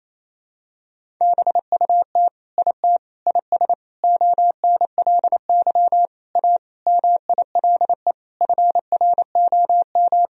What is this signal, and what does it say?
Morse code, heard as a keyed tone, about 27 wpm, 700 Hz. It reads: BUT IT IS ONLY A MILE FROM
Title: Sentences from Top 1000 Words - Review Part 1 - 28wpm